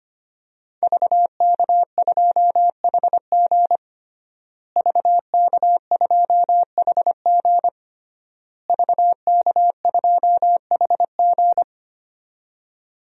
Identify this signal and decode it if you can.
Morse code, heard as a keyed tone, about 25 wpm, 700 Hz. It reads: VK2HG VK2HG VK2HG
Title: DXCC-100 Call Signs - 5 Characters - Repeated 3x - 25wpm